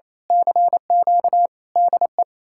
Morse code, heard as a keyed tone, about 30 wpm, 700 Hz. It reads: CQ DE
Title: Speed-Racing Common QSO Phrases - 28wpm